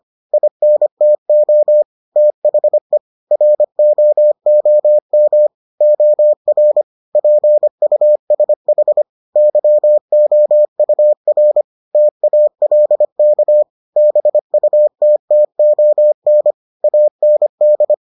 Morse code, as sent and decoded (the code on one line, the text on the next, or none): .. -. - --- / - .... / .-. --- --- -- / --- .-. / .--. ..- ... .... / -.-- --- ..- .-. / - .- .-.. -.- / -... ..- - - --- -. / .- -. -..
INTO THE ROOM OR PUSH YOUR TALK BUTTON AND